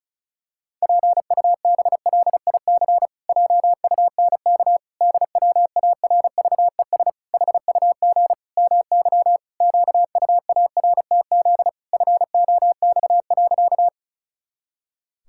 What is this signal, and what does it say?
Morse code, heard as a keyed tone, about 34 wpm, 700 Hz. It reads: PUBLIC JUNK DWARVES HUG MY QUARTZ FOX.
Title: Pangram Challenge v2 - 35wpm